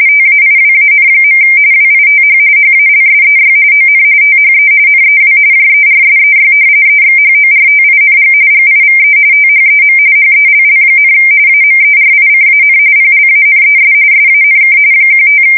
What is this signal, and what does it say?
RTTY, about 45.45 baud. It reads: TAIWAN. WITH HALSEY ON THE NEW JERSEY EXHORTING GIVE THEM HELL, AMERICAN PLANES SANK 15
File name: fsk170 / battleshipnj